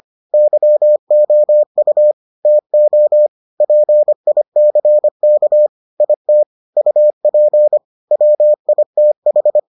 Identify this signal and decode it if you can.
Morse code, heard as a keyed tone, about 25 wpm, 600 Hz. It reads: YOU TO PICK IT UP WITH